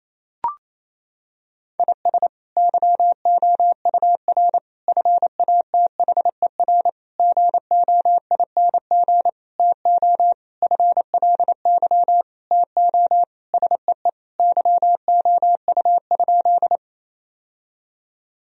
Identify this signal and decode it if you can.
Morse code, heard as a keyed tone, about 28 wpm, 700 Hz. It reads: IS YOUR FATHER GOING TO FLY TO SEE YOU?
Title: Sentences from Top 1000 Words - Review Part 2 - 28wpm